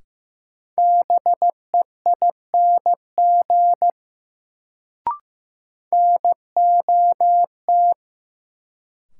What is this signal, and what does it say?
Morse code, heard as a keyed tone, about 15 wpm, 700 Hz. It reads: BEING NOT
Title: Sentences from Top 600 Words - New Word Drill - 15wpm